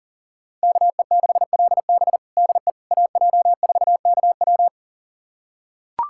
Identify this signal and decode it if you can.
Morse code, heard as a keyed tone, about 40 wpm, 700 Hz. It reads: KE6LB DE AJ4KW
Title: USA Callsign DE Callsign - 5 Characters - 40wpm